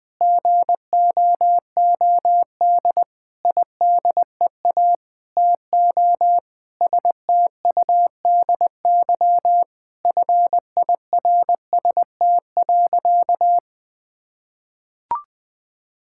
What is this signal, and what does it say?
Morse code, heard as a keyed tone, about 21 wpm, 700 Hz. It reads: GOOD IDEA TO STUDY FIRST.
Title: Sentences with 7-Word Phrases - 20wpm